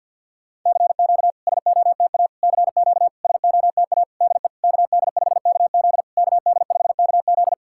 Morse code, heard as a keyed tone, about 51 wpm, 690 Hz. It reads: CQ SOTA CQ SOTA DE KD5KZ KD5KZ
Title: SOTA QSO Practice 50wpm